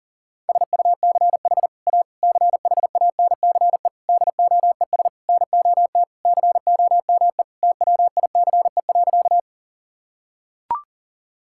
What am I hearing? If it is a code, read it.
Morse code, heard as a keyed tone, about 37 wpm, 700 Hz. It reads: SUCH A CHANCE DOES NOT COME TWICE.
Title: Sentences from Top 1600 Words 40wpm